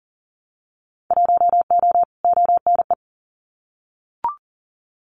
Morse code, heard as a keyed tone, about 40 wpm, 700 Hz. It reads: 1O ONE
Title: Winter Field Day Exchanges 40wpm